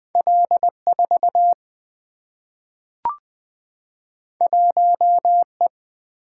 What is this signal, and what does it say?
Morse code, heard as a keyed tone, about 20 wpm, 700 Hz. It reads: L4 1E
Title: Two Letter-Number Combinations - 20wpm